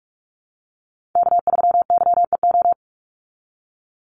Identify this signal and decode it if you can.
Morse code, heard as a keyed tone, about 44 wpm, 700 Hz. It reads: K3YEO
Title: Call Signs - 45wpm